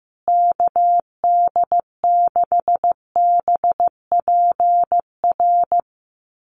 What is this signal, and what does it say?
Morse code, heard as a keyed tone, about 15 wpm, 700 Hz. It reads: KD6BPR
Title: USA Call Signs Repeated 2x 15wpm